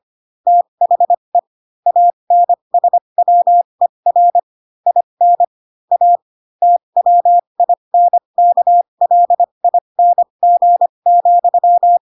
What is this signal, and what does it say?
Morse code, heard as a keyed tone, about 25 wpm, spaced as for 22 wpm, 700 Hz. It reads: THE ANSWER IN A TWINKLING,